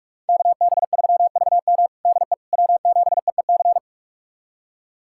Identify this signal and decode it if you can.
Morse code, heard as a keyed tone, about 41 wpm, 700 Hz. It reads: KB3VM DE W7EEC
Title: USA Callsign DE Callsign - 5 Characters - 45wpm